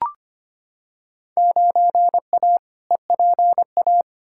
Morse code, heard as a keyed tone, about 25 wpm, 700 Hz. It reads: 9A EPA
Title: Field Day Exchanges Encore - 25wpm